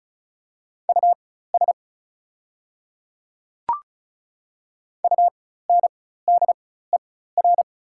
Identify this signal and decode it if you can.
Morse code, heard as a keyed tone, about 34 wpm, spaced as for 16 wpm, 700 Hz. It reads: US UNDER
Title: 4x Character Spacing - Top 200 Words - 35wpm